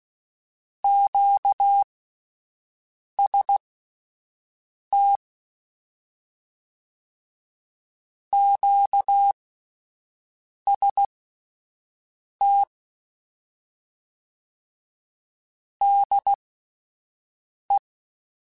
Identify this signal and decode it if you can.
Morse code, heard as a keyed tone, about 16 wpm, spaced as for 5 wpm, 800 Hz. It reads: QST QST DE
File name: Misc / Misc-mp3s / QST QST @10